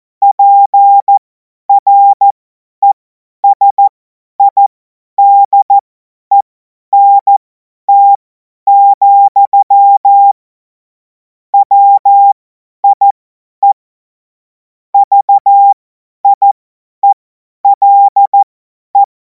Morse code, heard as a keyed tone, about 14 wpm, spaced as for 10 wpm, 800 Hz. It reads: PRESIDENT, WIE VIELE
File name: Morsekurs / Witze / Witz 12. Mai 2026 50 F70